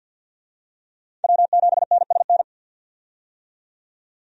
Morse code, heard as a keyed tone, about 49 wpm, 700 Hz. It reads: W7NSN
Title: USA Call Signs - 5 Characters - 50wpm